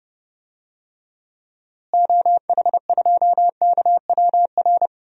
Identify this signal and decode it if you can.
Morse code, heard as a keyed tone, about 32 wpm, 700 Hz. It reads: OH2KWR
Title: DXCC-100 Call Signs - 6 Characters - Repeated 2x - 30wpm